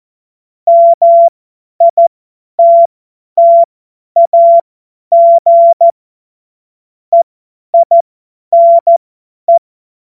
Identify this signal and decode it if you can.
Morse code, heard as a keyed tone, about 14 wpm, spaced as for 10 wpm, 680 Hz. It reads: MITTAG EINE
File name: Morsekurs / Witze / Witz 26 März 2024 50 F70